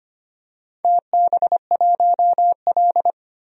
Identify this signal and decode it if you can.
Morse code, heard as a keyed tone, about 25 wpm, 690 Hz. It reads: TB1L
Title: DXCC-100 Call Signs - 4 Characters - 25wpm